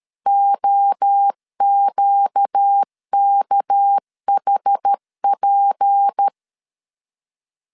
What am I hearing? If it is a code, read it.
Morse code, heard as a keyed tone, about 13 wpm, 800 Hz. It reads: OQKHP
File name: LEZIONI / 6 PKX0 / 6 PK 12wpm 2006